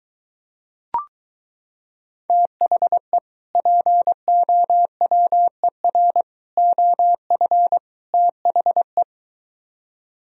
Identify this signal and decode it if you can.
Morse code, heard as a keyed tone, about 23 wpm, 700 Hz. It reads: THE POWER OF THE